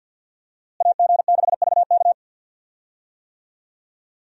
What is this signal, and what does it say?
Morse code, heard as a keyed tone, about 47 wpm, 700 Hz. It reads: AG6VK
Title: USA Call Signs - 5 Characters - Repeated 3x - 50wpm